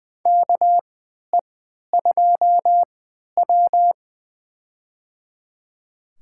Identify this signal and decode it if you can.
Morse code, heard as a keyed tone, about 21 wpm, spaced as for 12 wpm, 700 Hz. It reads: KE2W
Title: USA Call Signs - 3x Character Spacing - 20wpm